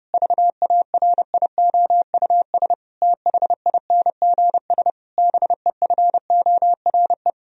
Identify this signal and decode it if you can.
Morse code, heard as a keyed tone, about 30 wpm, 700 Hz. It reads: VARIOUS THINGS BEFORE
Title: Sentences from Top 1000 Words - Review Part 4 - 30wpm